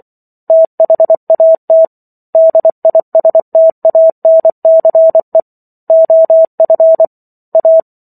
Morse code, heard as a keyed tone, about 24 wpm, 650 Hz. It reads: THAT DISTANCE OF A
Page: War of the Worlds - 15-Chapter 15 - 24 WPM